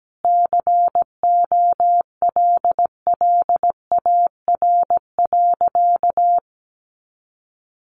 Morse code, heard as a keyed tone, about 17 wpm, 700 Hz. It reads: COLLAR.
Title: Easy Phrases 17wpm